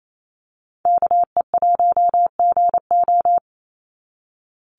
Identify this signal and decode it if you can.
Morse code, heard as a keyed tone, about 28 wpm, 700 Hz. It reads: KE1GO
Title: USA Call Signs 28wpm